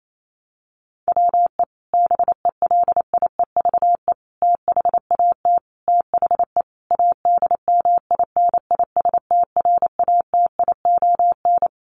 Morse code, heard as a keyed tone, about 28 wpm, 700 Hz. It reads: WE BELIEVE THAT THE ADMINISTRATION
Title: Sentences from Top 1300 Words 28wpm